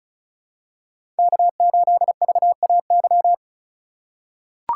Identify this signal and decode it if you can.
Morse code, heard as a keyed tone, about 36 wpm, 700 Hz. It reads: K8VAY
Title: USA Call Signs - 3x Character Spacing - 35wpm